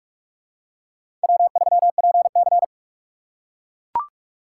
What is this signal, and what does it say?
Morse code, heard as a keyed tone, about 43 wpm, 700 Hz. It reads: W3PC
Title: USA Call Signs - 4 Characters - 4x Character Spacing - 45wpm